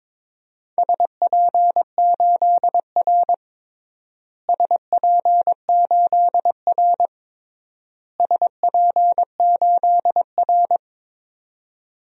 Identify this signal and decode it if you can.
Morse code, heard as a keyed tone, about 22 wpm, 700 Hz. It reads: SP8R SP8R SP8R
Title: DXCC-100 Call Signs - 4 Characters - Repeated 3x - 22wpm